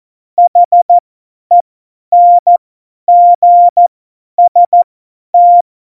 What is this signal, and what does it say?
Morse code, heard as a keyed tone, about 14 wpm, spaced as for 10 wpm, 700 Hz. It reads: HENGST
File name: Morsekurs / Witze / Witz 15. Dezember 2020, 50 F70